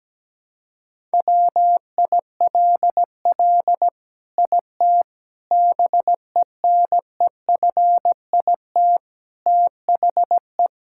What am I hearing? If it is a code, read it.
Morse code, heard as a keyed tone, about 17 wpm, 700 Hz. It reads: WILL IT BENEFIT THE
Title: Sentences from Top 1000 Words - Review Part 6 - 17wpm